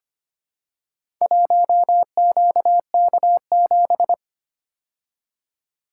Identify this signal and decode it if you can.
Morse code, heard as a keyed tone, about 25 wpm, 700 Hz. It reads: 1QK7
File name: practice / 25wpm / Four Letter-Number Combinations - 25wpm